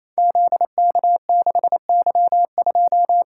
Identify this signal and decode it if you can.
Morse code, heard as a keyed tone, about 28 wpm, 690 Hz. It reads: ZK6Y2
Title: Code Groups - 5 Letter:Numbers - 28wpm